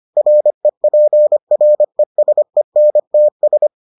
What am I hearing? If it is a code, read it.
Morse code, heard as a keyed tone, about 25 wpm, 600 Hz. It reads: REPRESENTS